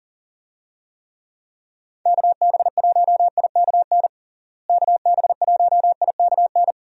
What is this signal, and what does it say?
Morse code, heard as a keyed tone, about 37 wpm, 700 Hz. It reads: KB1IKN KB1IKN
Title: USA Call Signs - 6 Characters - Repeated 2x - 40wpm